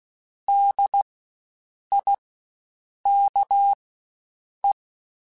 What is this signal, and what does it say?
Morse code, heard as a keyed tone, about 16 wpm, spaced as for 7 wpm, 800 Hz. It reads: DIKE
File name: Spaced Alphas / Spaced Alphas mp3s / Ds sp @15